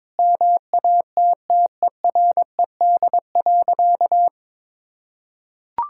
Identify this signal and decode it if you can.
Morse code, heard as a keyed tone, about 22 wpm, 700 Hz. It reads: MATTERED.
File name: practice / 22wpm / Sentences with 4-Word Phrases - 22wpm